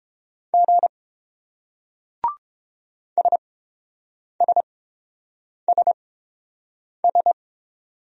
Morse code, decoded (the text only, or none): G SSSS